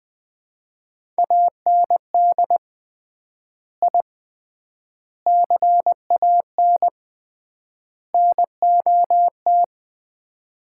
Morse code, heard as a keyed tone, about 20 wpm, 700 Hz. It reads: AND I CAN NOT
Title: 3x Word-Spacing - Sets of 4 Words - 20wpm